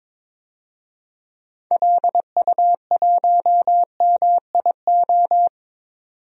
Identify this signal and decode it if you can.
Morse code, heard as a keyed tone, about 21 wpm, 700 Hz. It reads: LU1MIO
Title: DXCC-100 Call Signs - 6 Characters - 22wpm